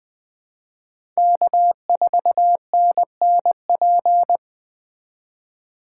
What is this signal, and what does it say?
Morse code, heard as a keyed tone, about 20 wpm, 700 Hz. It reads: K4NNP
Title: Speed Racing - USA 1x3 Call Signs 17wpm